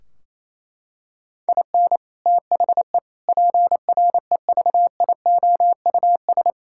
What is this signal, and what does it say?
Morse code, heard as a keyed tone, about 28 wpm, 700 Hz. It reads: IN THE PREVIOUS
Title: Sentences from Top 1600 Words 28wpm